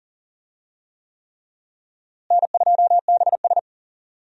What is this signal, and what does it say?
Morse code, heard as a keyed tone, about 40 wpm, 700 Hz. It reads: N2BS
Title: USA Call Signs - 4 Characters - Repeated 3x - 40wpm